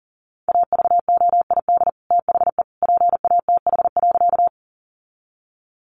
Morse code, heard as a keyed tone, about 37 wpm, 700 Hz. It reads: AVOID THE PATH.
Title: Sentences from Top 1400 Words 40wpm